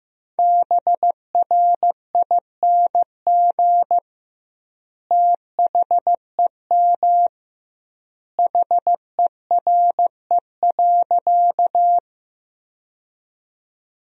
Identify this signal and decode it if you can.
Morse code, heard as a keyed tone, about 15 wpm, 700 Hz. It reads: BRING THEM HERE.
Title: 2x Word-Spacing - Sentences from Top 500 Words - 15wpm